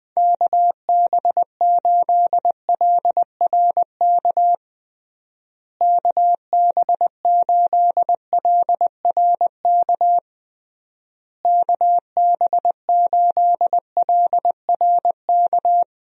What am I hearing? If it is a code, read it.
Morse code, heard as a keyed tone, about 20 wpm, 700 Hz. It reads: KB8LRK KB8LRK KB8LRK
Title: USA Call Signs - 6 Characters - Repeated 3x - 20wpm